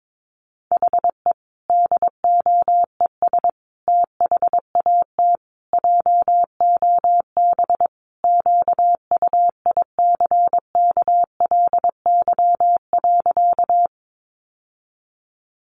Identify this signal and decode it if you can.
Morse code, heard as a keyed tone, about 22 wpm, 700 Hz. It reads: HE DOES THAT JOB QUICKLY.
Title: Sentences from Top 700 Words 22wpm